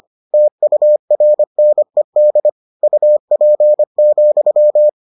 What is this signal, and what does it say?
Morse code, heard as a keyed tone, about 25 wpm, 600 Hz. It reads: TURNED UP,